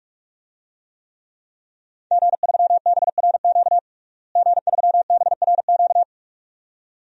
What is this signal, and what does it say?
Morse code, heard as a keyed tone, about 44 wpm, 700 Hz. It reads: G3BRQ G3BRQ
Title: DXCC-100 Call Signs - 5 Characters - Repeated 2x - 45wpm